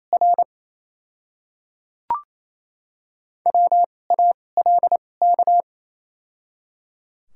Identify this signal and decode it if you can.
Morse code, heard as a keyed tone, about 28 wpm, spaced as for 20 wpm, 700 Hz. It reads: R WALK
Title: 2x Character Spacing - Top 500 Words - 28wpm